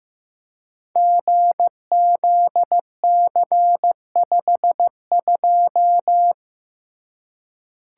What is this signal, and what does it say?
Morse code, heard as a keyed tone, about 15 wpm, 700 Hz. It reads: GZC52
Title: Code Groups - 5 Letter:Numbers - 15wpm